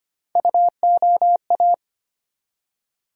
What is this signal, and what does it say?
Morse code, heard as a keyed tone, about 25 wpm, 700 Hz. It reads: UOA
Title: USA Call Sign Postfixes - 3 Letters 25wpm